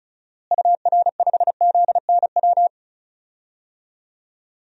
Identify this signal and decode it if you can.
Morse code, heard as a keyed tone, about 36 wpm, 700 Hz. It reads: UR5ZNW
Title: DXCC-100 Call Signs - 6 Characters - Repeated 3x - 35wpm